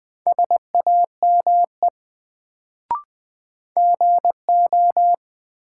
Morse code, heard as a keyed tone, about 20 wpm, 700 Hz. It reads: SAME GO
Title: Sets of 2 Words - New Word Drill - 20wpm